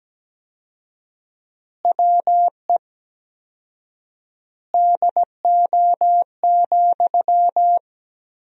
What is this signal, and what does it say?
Morse code, heard as a keyed tone, about 17 wpm, 700 Hz. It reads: WE DO,
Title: 4x Word-Spacing - Sentences from Top 100 Words - 17wpm